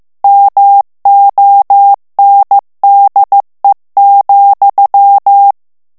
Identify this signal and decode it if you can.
Morse code, heard as a keyed tone, about 15 wpm, 800 Hz. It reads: MONDE,
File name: FRANCAIS TEXTE CLAIR 2 VITESSE 14 MOTS MINUTES